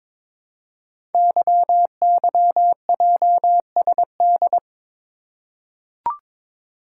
Morse code, heard as a keyed tone, about 22 wpm, 690 Hz. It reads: YYJSD
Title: Code Groups - 5 Letters - 22wpm